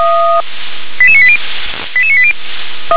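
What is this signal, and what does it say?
Five-tone selective call (ZVEI), heard as quick runs of five tones, tones of about 70 ms. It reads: 88980 88980
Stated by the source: ZVEI five-tone selective call 88771, received by SDR.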